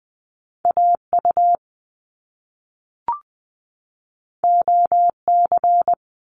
Morse code, heard as a keyed tone, about 20 wpm, 700 Hz. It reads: AU OC
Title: Top Two Letter Combinations in Words 20wpm